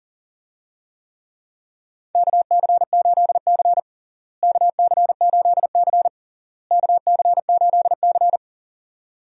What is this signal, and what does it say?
Morse code, heard as a keyed tone, about 37 wpm, 700 Hz. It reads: KC8C KC8C KC8C
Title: USA Call Signs - 4 Characters - Repeated 3x - 40wpm